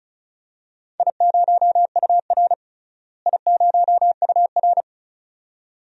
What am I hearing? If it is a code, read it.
Morse code, heard as a keyed tone, about 36 wpm, 700 Hz. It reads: I0UR I0UR
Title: DXCC-100 Call Signs - 4 Characters - Repeated 2x - 35wpm